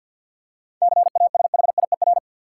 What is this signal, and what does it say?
Morse code, heard as a keyed tone, about 46 wpm, 700 Hz. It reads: CASHIER
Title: Practice QSO - Exchange 3 - More Info - 50wpm